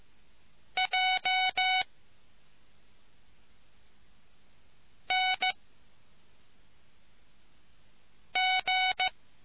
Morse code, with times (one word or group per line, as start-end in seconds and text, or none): 0.8-1.8 J
5.1-5.5 N
8.4-9.1 G